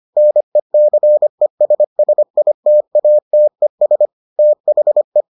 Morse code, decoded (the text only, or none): NECESSITATES THE